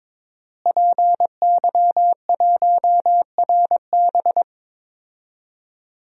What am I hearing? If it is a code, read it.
Morse code, heard as a keyed tone, about 22 wpm, 700 Hz. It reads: PY1RB
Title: DXCC-100 Call Signs - 5 Characters - Repeated 2x - 22wpm